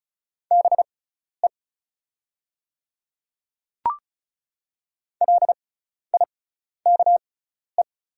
Morse code, heard as a keyed tone, about 36 wpm, spaced as for 12 wpm, 700 Hz. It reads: BE LIKE
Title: 6x Character Spacing - Top 100 Words - 35wpm